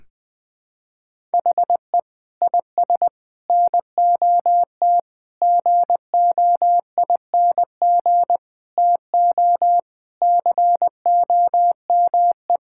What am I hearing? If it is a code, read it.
Morse code, heard as a keyed tone, about 21 wpm, 700 Hz. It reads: HE IS NOT GOING TO COME
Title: Sentences with 4-Word Phrases - 20wpm